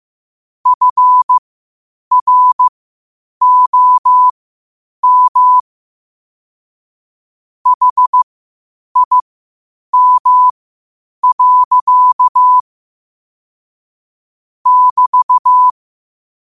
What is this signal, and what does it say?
Morse code, heard as a keyed tone, about 15 wpm, spaced as for 8 wpm, 1000 Hz. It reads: FROM HIM. <BT>
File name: qotdfiles / 8wpm